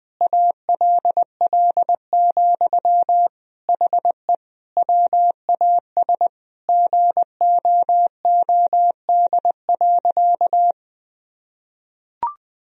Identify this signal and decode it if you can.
Morse code, heard as a keyed tone, about 20 wpm, 700 Hz. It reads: ALL, HE WAS GOOD.
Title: Sentences from Top 1000 Words - Review Part 2 - 20wpm